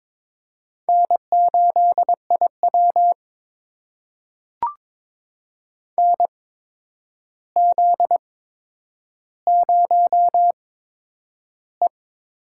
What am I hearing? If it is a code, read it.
Morse code, heard as a keyed tone, about 22 wpm, spaced as for 9 wpm, 700 Hz. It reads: N8IW NZ0E
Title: USA Call Signs - 4 Characters - 8x Character Spacing - 22wpm